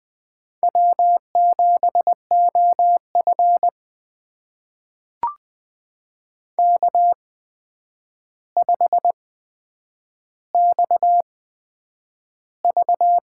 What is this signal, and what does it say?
Morse code, heard as a keyed tone, about 20 wpm, spaced as for 9 wpm, 700 Hz. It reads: W7OF K5XV
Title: USA Call Signs - 4 Characters - 8x Character Spacing - 20wpm